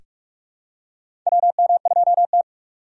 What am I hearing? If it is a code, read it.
Morse code, heard as a keyed tone, about 46 wpm, 700 Hz. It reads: WM2T
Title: USA Call Signs - 4 Characters - 4x Character Spacing - 45wpm